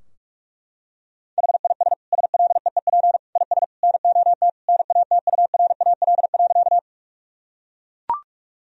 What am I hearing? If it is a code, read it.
Morse code, heard as a keyed tone, about 42 wpm, 700 Hz. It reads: HIS SLEEP IS NOT NATURAL.
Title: Sentences from Top 1000 Words - Review Part 4 - 45wpm